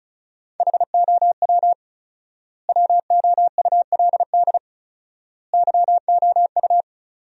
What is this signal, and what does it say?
Morse code, heard as a keyed tone, about 36 wpm, 700 Hz. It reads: HOW WOULD YOU
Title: 4x Word-Spacing - Sentences from Top 100 Words - 35wpm